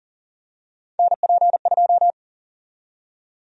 Morse code, heard as a keyed tone, about 40 wpm, 700 Hz. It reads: NP2
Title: USA Call Sign Prefixes - 40wpm